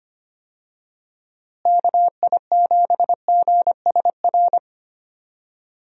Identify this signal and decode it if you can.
Morse code, heard as a keyed tone, about 25 wpm, 700 Hz. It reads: KI7GSR
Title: USA Call Signs - 6 Characters - Repeated 3x - 25wpm